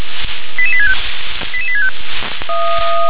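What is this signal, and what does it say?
Five-tone selective call (ZVEI), heard as quick runs of five tones, tones of about 80 ms. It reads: 88865 88865
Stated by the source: ZVEI five-tone selective call received on SDR.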